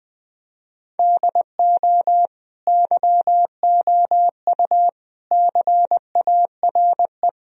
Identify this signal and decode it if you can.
Morse code, heard as a keyed tone, about 20 wpm, 700 Hz. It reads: DO YOU CARE
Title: Sentences from Top 1000 Words - Review Part 1 - 20wpm